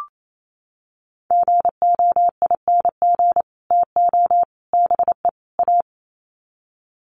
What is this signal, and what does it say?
Morse code, heard as a keyed tone, about 28 wpm, 700 Hz. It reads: GOING TO BE A
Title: Sets of 4 Words 28wpm